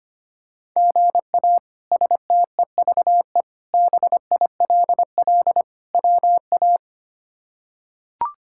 Morse code, heard as a keyed tone, about 25 wpm, 700 Hz. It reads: GA STEVE BILL WA
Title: K1USN SST QSO Practice - 25wpm